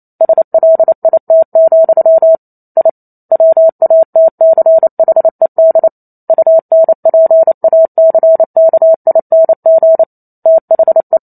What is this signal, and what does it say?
Morse code, heard as a keyed tone, about 28 wpm, 650 Hz. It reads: SLIT, I WATCHED UNPACKING THE
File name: Chapter 2 - 28 WPM